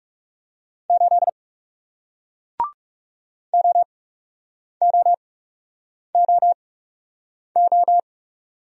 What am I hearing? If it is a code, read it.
Morse code, heard as a keyed tone, about 40 wpm, spaced as for 8 wpm, 700 Hz. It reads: 8 OOOO